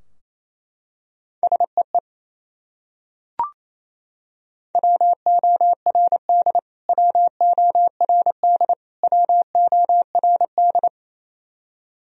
Morse code, heard as a keyed tone, about 28 wpm, 700 Hz. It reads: SEE WORD WORD WORD
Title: Top 100 Words Repeated 28wpm